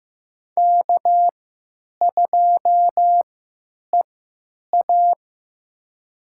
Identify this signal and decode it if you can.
Morse code, heard as a keyed tone, about 15 wpm, spaced as for 8 wpm, 700 Hz. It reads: K2EA
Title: USA Call Signs - 4 Characters - 3x Character Spacing - 15wpm